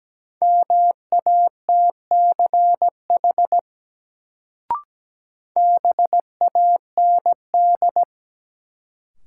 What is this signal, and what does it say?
Morse code, heard as a keyed tone, about 17 wpm, 700 Hz. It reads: MATCH BAND
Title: Sentences from Top 1600 Words - New Word Drill Alt - 17wpm